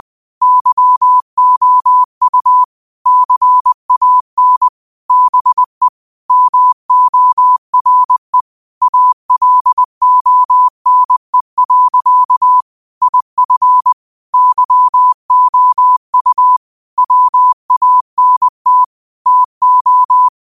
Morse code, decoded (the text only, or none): YOU CAN BE MORE ALONE. IF YOU WANT TO